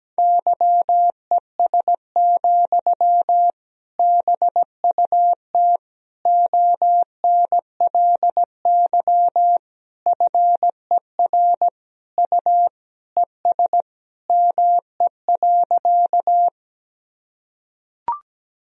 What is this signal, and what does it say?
Morse code, heard as a keyed tone, about 17 wpm, 700 Hz. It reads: YES, BUT ONLY FER U ES ME.
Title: Sentences from Top 500 Words and Abbreviations 17wpm